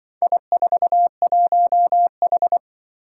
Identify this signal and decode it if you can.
Morse code, heard as a keyed tone, about 24 wpm, 700 Hz. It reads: I41H
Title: Speed-Racing - Four Letter-Number Combinations - 20wpm